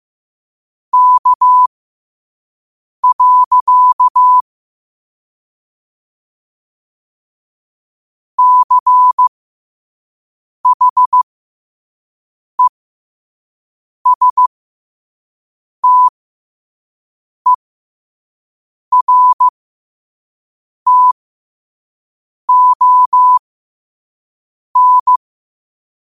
Morse code, heard as a keyed tone, about 15 wpm, spaced as for 5 wpm, 1000 Hz. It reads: K. CHESTERTON
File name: qotdfiles / 5wpm